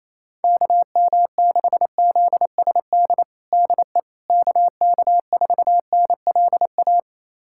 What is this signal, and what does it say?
Morse code, heard as a keyed tone, about 28 wpm, 700 Hz. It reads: KM6ZSD DE KK4NLA
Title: USA Callsign DE Callsign - 6 Characters - 28wpm